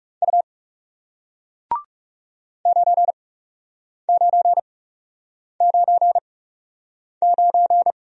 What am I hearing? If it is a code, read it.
Morse code, heard as a keyed tone, about 39 wpm, spaced as for 8 wpm, 700 Hz. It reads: U 9999